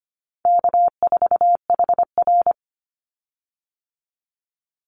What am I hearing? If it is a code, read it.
Morse code, heard as a keyed tone, about 25 wpm, 700 Hz. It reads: K4HR
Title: Call Signs Repeated 3x - 25wpm